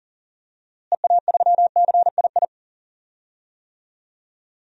Morse code, heard as a keyed tone, about 40 wpm, 700 Hz. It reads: EA3CII
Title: DXCC-100 Call Signs - 6 Characters - Repeated 2x - 40wpm